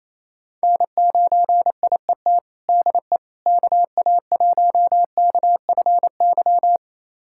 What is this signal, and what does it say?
Morse code, heard as a keyed tone, about 28 wpm, 700 Hz. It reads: N9IET DE KA1KFY
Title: USA Callsign DE Callsign 28wpm